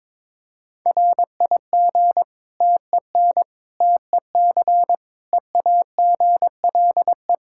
Morse code, heard as a keyed tone, about 22 wpm, 700 Hz. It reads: RIG TEN TEC EAGLE
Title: Practice QSO - Exchange 2 - Info - 22wpm